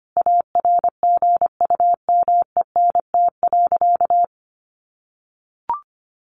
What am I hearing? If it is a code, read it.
Morse code, heard as a keyed tone, about 25 wpm, 700 Hz. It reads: ARGUMENT.
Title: Sentences from Top 1400 Words 25wpm